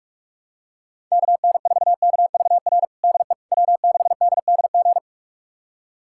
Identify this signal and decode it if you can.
Morse code, heard as a keyed tone, about 42 wpm, 700 Hz. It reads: KN4KVR DE W6DDG
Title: USA Callsign DE Callsign 45wpm